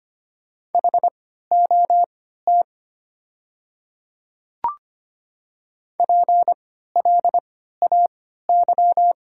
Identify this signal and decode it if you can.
Morse code, heard as a keyed tone, about 25 wpm, spaced as for 14 wpm, 700 Hz. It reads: HOT PLAY
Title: 3x Character Spacing - Top 200 Words - 25wpm